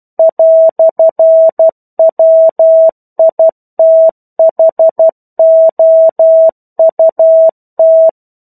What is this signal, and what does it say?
Morse code, heard as a keyed tone, about 12 wpm, 650 Hz. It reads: "WITHOUT
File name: 01-Chapter-01 12 WPM